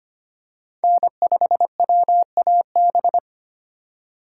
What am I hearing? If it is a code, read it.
Morse code, heard as a keyed tone, about 25 wpm, 700 Hz. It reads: N5WAB
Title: USA Call Signs - 5 Characters - 4x Character Spacing - 25wpm